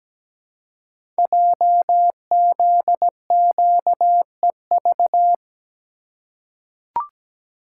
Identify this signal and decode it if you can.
Morse code, heard as a keyed tone, about 17 wpm, 700 Hz. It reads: JZQEV
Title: Code Groups - 5 Letters - 17wpm